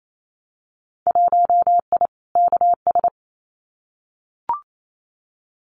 Morse code, heard as a keyed tone, about 28 wpm, 690 Hz. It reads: 1I KS
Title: Winter Field Day Exchanges Encore 28wpm